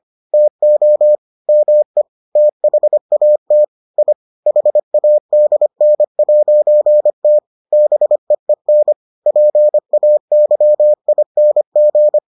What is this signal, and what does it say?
Morse code, heard as a keyed tone, about 25 wpm, 600 Hz. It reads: TO ME THAT I HADN'T BEEN PAYING